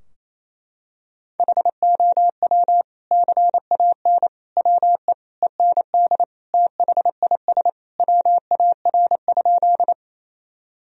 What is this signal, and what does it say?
Morse code, heard as a keyed tone, about 28 wpm, 700 Hz. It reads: HOW CAN WE END THIS WAR?
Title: Sentences from Top 1000 Words - Review Part 2 - 28wpm